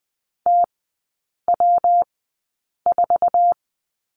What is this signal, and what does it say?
Morse code, heard as a keyed tone, about 20 wpm, spaced as for 8 wpm, 700 Hz. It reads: TW4
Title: 121 Lesson 20e - Number 3 - Three Characters 20wpm